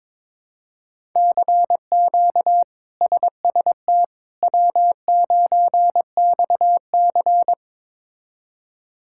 Morse code, heard as a keyed tone, about 22 wpm, 700 Hz. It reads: CQ SST W9XC
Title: K1USN SST QSO Practice - 22wpm